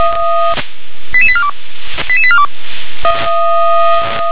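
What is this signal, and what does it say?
Five-tone selective call (ZVEI), heard as quick runs of five tones, tones of about 70 ms. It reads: 88842 88842
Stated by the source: Five-tone ZVEI selective call received on an SDR.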